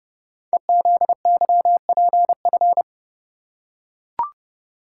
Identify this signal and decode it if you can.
Morse code, heard as a keyed tone, about 30 wpm, 690 Hz. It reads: EZYPF
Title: Code Groups - 5 Letters - 30wpm